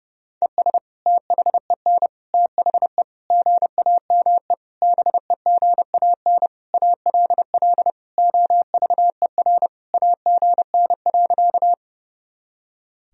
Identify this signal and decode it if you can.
Morse code, heard as a keyed tone, about 32 wpm, 700 Hz. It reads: ES THEN THE GAME BEGAN ALL OVER AGN.
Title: Sentences from Top 500 Words and Abbreviations 30wpm